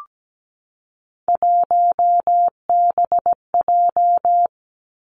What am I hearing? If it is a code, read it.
Morse code, heard as a keyed tone, about 17 wpm, 700 Hz. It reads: 1BJ
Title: Three Challenging Letter-Number Combinations 17wpm